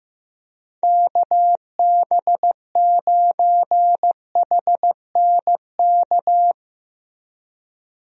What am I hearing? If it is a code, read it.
Morse code, heard as a keyed tone, about 15 wpm, 700 Hz. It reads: KB9HNK
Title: USA Call Signs - 6 Characters - Repeated 3x - 15wpm